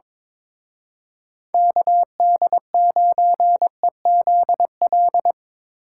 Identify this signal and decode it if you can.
Morse code, heard as a keyed tone, about 25 wpm, 700 Hz. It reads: KD9EZL
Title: Speed Racing - USA 2x3 Call Signs 22wpm